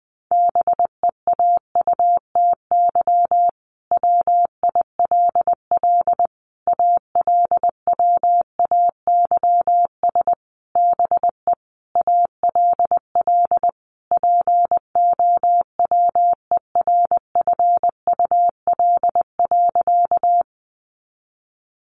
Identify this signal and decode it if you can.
Morse code, heard as a keyed tone, about 20 wpm, 700 Hz. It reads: BEAUTY WILL ALWAYS BE ALL POWERFUL.
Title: Sentences from Top 1300 Words 20wpm